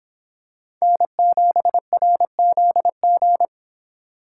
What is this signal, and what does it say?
Morse code, heard as a keyed tone, about 25 wpm, 700 Hz. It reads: N7RZG